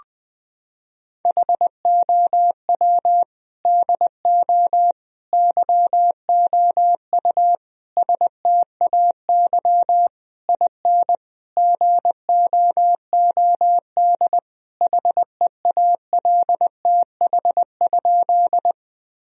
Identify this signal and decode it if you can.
Morse code, heard as a keyed tone, about 20 wpm, 700 Hz. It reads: HOW DO YOU STAY IN GOOD HEALTH?
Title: Sentences from Top 1000 Words - Review Part 3 - 20wpm